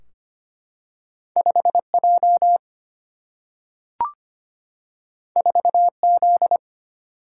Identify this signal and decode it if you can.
Morse code, heard as a keyed tone, about 25 wpm, 700 Hz. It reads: 5 J 4 Z
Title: Single-Letter Single-Number 25wpm